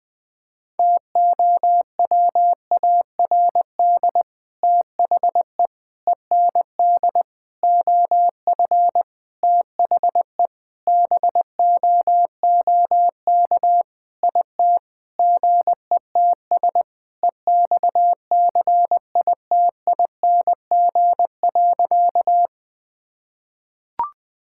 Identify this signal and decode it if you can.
Morse code, heard as a keyed tone, about 21 wpm, 700 Hz. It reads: TOWARD THE END OF THE BOOK IT GETS EXCITING.
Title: Sentences with 5-Word Phrases - Spoken Beginning - 20wpm